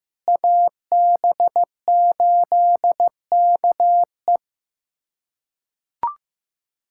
Morse code, heard as a keyed tone, about 15 wpm, 700 Hz. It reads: AB8KE
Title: USA Callsign DE Callsign - 5 Characters - 15wpm